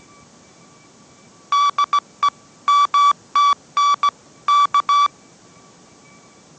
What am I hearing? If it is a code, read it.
Morse code, heard as a keyed tone, about 17 wpm, 1150 Hz. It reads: DE MTN K